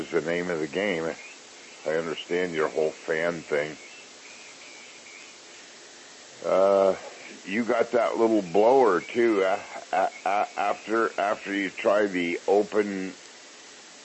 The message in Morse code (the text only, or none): none